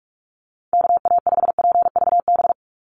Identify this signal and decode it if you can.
Morse code, heard as a keyed tone, about 44 wpm, 700 Hz. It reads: KA5PVB
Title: NA SOTA Chaser Callsigns 45wpm